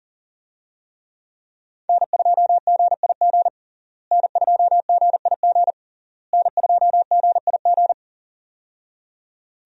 Morse code, heard as a keyed tone, about 40 wpm, 690 Hz. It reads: N2GIG N2GIG N2GIG
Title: USA Call Signs - 5 Characters - Repeated 3x - 40wpm